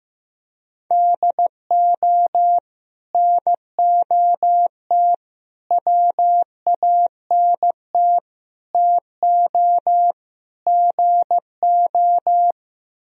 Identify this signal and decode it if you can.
Morse code, heard as a keyed tone, about 15 wpm, 700 Hz. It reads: DO NOT WANT TO GO
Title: Speed-Racing - Sets of 5 Words 15wpm